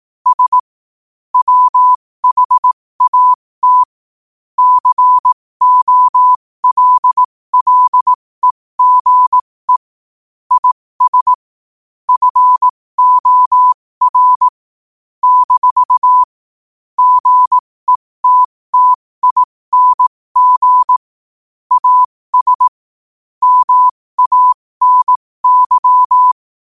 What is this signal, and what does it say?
Morse code, heard as a keyed tone, about 18 wpm, spaced as for 15 wpm, 1000 Hz. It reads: S WHAT COLLEGE IS FOR - GETTING AS MANY
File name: qotdfiles / 15wpm